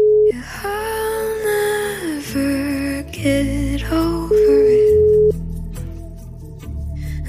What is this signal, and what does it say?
Ringback tone (Europe, 425 Hz).